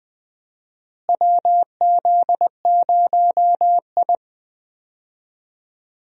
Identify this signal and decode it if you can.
Morse code, heard as a keyed tone, about 20 wpm, 700 Hz. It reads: WZ0I
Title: USA Call Signs - 4 Characters - Repeated 3x - 20wpm